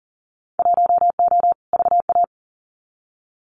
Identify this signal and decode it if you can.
Morse code, heard as a keyed tone, about 40 wpm, 700 Hz. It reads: 1O VA
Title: Winter Field Day Exchanges 40wpm